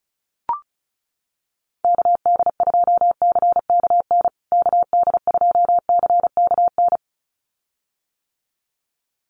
Morse code, heard as a keyed tone, about 34 wpm, 700 Hz. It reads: KD2CKN KD2CKN
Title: USA Call Signs Repeated 2x 35wpm